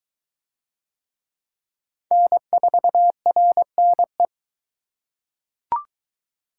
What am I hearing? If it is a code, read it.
Morse code, heard as a keyed tone, about 22 wpm, 700 Hz. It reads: N4RNE